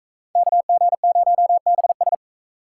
Morse code, heard as a keyed tone, about 44 wpm, 700 Hz. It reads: KG0BS